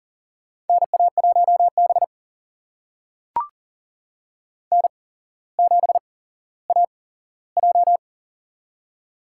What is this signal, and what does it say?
Morse code, heard as a keyed tone, about 43 wpm, spaced as for 17 wpm, 700 Hz. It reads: NA1B N7AJ
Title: USA Call Signs - 4 Characters - 8x Character Spacing - 40wpm